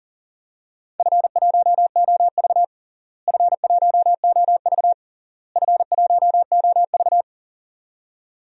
Morse code, heard as a keyed tone, about 40 wpm, 700 Hz. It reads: F1OV F1OV F1OV
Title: DXCC-100 Call Signs - 4 Characters - Repeated 3x - 40wpm